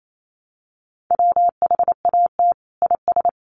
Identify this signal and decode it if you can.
Morse code, heard as a keyed tone, about 28 wpm, 700 Hz. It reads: WHAT IS
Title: Sentences from Top 1000 Words 28wpm